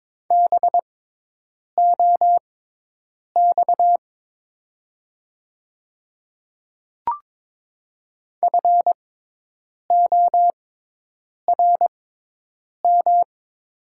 Morse code, heard as a keyed tone, about 22 wpm, spaced as for 8 wpm, 700 Hz. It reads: BOX FORM
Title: 6x Character Spacing - Top 500 Words - 22wpm